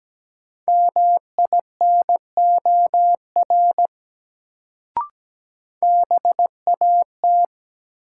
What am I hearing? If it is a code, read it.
Morse code, heard as a keyed tone, about 17 wpm, 700 Hz. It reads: MINOR BAT
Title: Sentences from Top 1500 Words - New Word Drill Alt - 17wpm